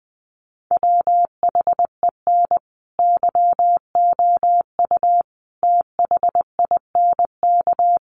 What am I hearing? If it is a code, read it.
Morse code, heard as a keyed tone, about 20 wpm, 700 Hz. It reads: WHEN YOU THINK
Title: Sets of 5 Words Encore2 20wpm